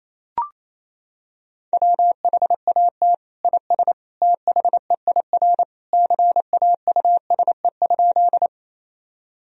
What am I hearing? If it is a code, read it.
Morse code, heard as a keyed tone, about 28 wpm, 700 Hz. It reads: WHAT IS THEIR CAUSE?
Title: Sentences from Top 1000 Words - Review Part 1 - 28wpm